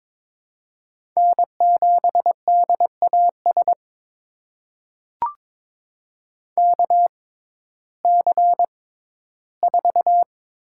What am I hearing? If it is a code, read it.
Morse code, heard as a keyed tone, about 22 wpm, spaced as for 13 wpm, 700 Hz. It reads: N7DAS KC4
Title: USA Call Signs - 6x Character Spacing - 22wpm